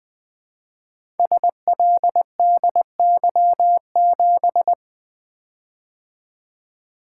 Morse code, heard as a keyed tone, about 20 wpm, 700 Hz. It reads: SLDY7
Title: Code Groups - 5 Letter:Numbers - 20wpm